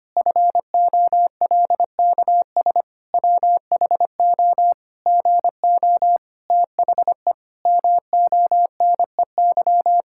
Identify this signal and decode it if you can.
Morse code, heard as a keyed tone, about 25 wpm, 700 Hz. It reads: FOLKS WHO GO THE MONEY
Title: Pangram Challenge v2 - 25wpm